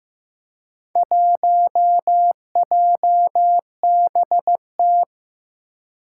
Morse code, heard as a keyed tone, about 15 wpm, 700 Hz. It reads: 1JBT
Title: Four Challenging Letter-Number Combinations 15wpm